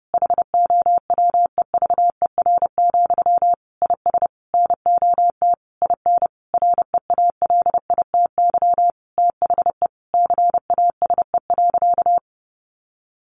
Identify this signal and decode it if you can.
Morse code, heard as a keyed tone, about 30 wpm, 700 Hz. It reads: HOWEVER, IS NOT IN REALITY THE CASE.
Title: Sentences from Top 1200 Words 30wpm